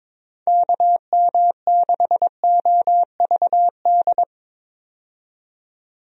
Morse code, heard as a keyed tone, about 22 wpm, 700 Hz. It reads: KM6OVD
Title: USA Call Signs - 6 Characters - 22wpm